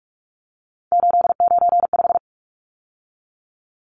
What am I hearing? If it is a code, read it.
Morse code, heard as a keyed tone, about 44 wpm, 700 Hz. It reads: 895
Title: Numbers - 3-Digit 45wpm